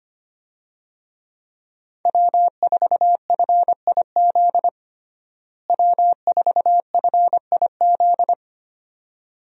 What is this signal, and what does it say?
Morse code, heard as a keyed tone, about 25 wpm, 700 Hz. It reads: W4FIZ W4FIZ
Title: USA Call Signs - 5 Characters - Repeated 2x - 25wpm